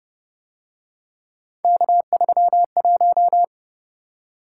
Morse code, heard as a keyed tone, about 30 wpm, 700 Hz. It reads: K31